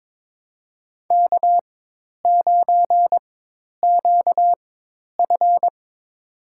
Morse code, heard as a keyed tone, about 22 wpm, spaced as for 10 wpm, 700 Hz. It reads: K9QF
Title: USA Call Signs - 4 Characters - 4x Character Spacing - 22wpm